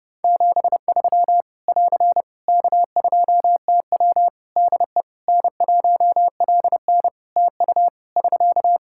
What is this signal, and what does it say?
Morse code, heard as a keyed tone, about 29 wpm, 700 Hz. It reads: 73 <AR> K2TW DE N1LN TU <SK>
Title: Practice QSO - Exchange 4 - Ending - 30wpm